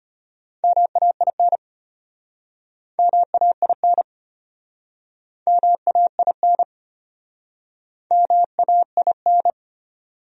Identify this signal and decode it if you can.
Morse code, heard as a keyed tone, about 33 wpm, 700 Hz. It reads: MAIN MAIN MAIN MAIN